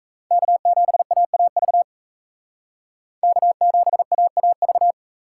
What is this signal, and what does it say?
Morse code, heard as a keyed tone, about 44 wpm, 700 Hz. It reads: K7AAV K7AAV